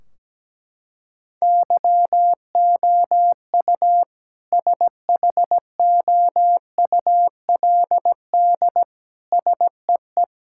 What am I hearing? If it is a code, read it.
Morse code, heard as a keyed tone, about 17 wpm, 700 Hz. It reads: YOU SHOULD SEE
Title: Sentences from Top 1600 Words 17wpm